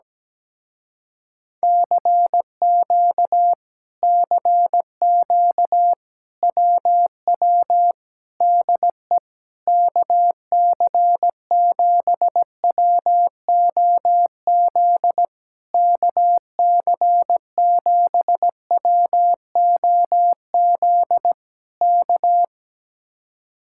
Morse code, heard as a keyed tone, about 17 wpm, 700 Hz. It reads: CQ CQ WW DE KC7WOZ KC7WOZ K
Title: Speed-Racing Common QSO Phrases - 17wpm